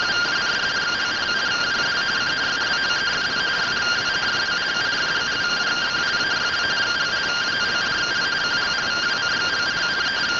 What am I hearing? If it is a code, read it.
Two-tone teleprinter signal, about 170 Hz shift.